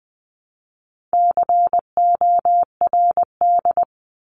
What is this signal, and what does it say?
Morse code, heard as a keyed tone, about 20 wpm, 700 Hz. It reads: CORD
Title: 260 Lesson 42e - Words 20-20wpm